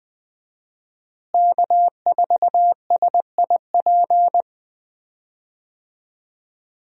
Morse code, heard as a keyed tone, about 20 wpm, 700 Hz. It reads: K4SIP
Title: USA Call Signs - 5 Characters - Repeated 2x - 20wpm